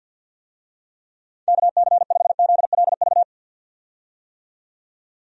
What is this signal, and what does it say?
Morse code, heard as a keyed tone, about 51 wpm, 700 Hz. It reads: KC5ZLV
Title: USA Call Signs - 6 Characters - 50wpm